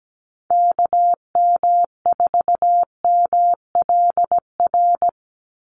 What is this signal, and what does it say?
Morse code, heard as a keyed tone, about 17 wpm, 700 Hz. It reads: KM4MLR
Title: USA Call Signs 17wpm